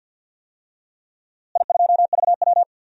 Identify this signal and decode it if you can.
Morse code, heard as a keyed tone, about 51 wpm, 700 Hz. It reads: I2VW
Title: DXCC-100 Call Signs - 4 Characters - Repeated 2x - 50wpm